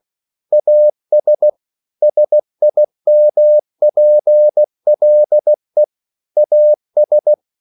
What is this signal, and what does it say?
Morse code, heard as a keyed tone, about 16 wpm, 600 Hz. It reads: AS SIMPLE AS
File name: CW / Practice / EnglishQuotations0216